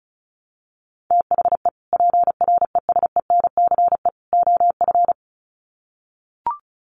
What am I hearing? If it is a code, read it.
Morse code, heard as a keyed tone, about 34 wpm, 700 Hz. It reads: THE PRESENCE OF
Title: Sets of 3 Words 35wpm